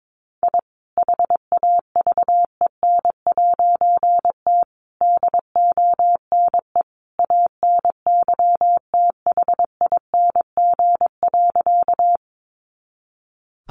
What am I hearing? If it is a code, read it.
Morse code, heard as a keyed tone, about 22 wpm, 700 Hz. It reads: I HAVEN'T DONE ANYTHING.
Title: A Sound Of Thunder 22wpm